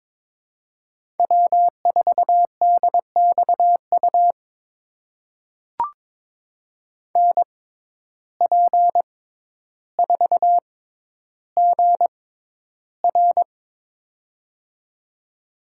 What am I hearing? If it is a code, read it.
Morse code, heard as a keyed tone, about 22 wpm, spaced as for 11 wpm, 700 Hz. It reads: W4DXU NP4GR
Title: USA Call Signs - 5 Characters - 6x Character Spacing - 22wpm